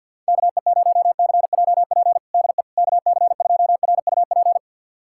Morse code, heard as a keyed tone, about 47 wpm, 700 Hz. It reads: KE0CJP DE KC2RUP
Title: USA Callsign DE Callsign - 6 Characters - 50wpm